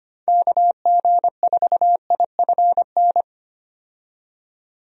Morse code, heard as a keyed tone, about 25 wpm, 700 Hz. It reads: KG4IFN
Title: USA Call Signs - 6 Characters - 25wpm